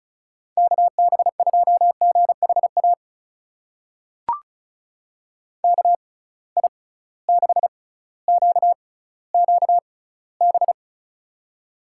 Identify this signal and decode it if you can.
Morse code, heard as a keyed tone, about 36 wpm, spaced as for 18 wpm, 700 Hz. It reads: KB2GHA KI6QQB
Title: USA Call Signs - 6 Characters - 6x Character Spacing - 35wpm